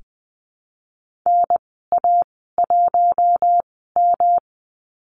Morse code, heard as a keyed tone, about 20 wpm, spaced as for 15 wpm, 700 Hz. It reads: NA1M
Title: 173 Lesson 28g - Number 9 - Call Signs 20wpm